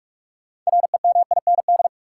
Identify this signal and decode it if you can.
Morse code, heard as a keyed tone, about 46 wpm, 700 Hz. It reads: REMIND
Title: Six Letter Words - 45wpm